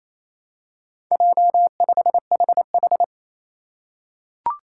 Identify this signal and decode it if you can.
Morse code, heard as a keyed tone, about 28 wpm, 700 Hz. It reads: J5HH
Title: Four Challenging Letter-Number Combinations 28wpm